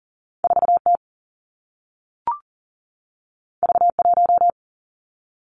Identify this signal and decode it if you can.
Morse code, heard as a keyed tone, about 39 wpm, 700 Hz. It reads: 4 T V 1
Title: Two Challenging Letter-Number Combinations 40wpm